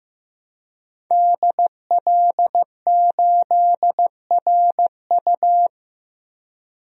Morse code, heard as a keyed tone, about 15 wpm, 700 Hz. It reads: DL8RU
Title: DXCC-100 Call Signs - 5 Characters - Repeated 2x - 15wpm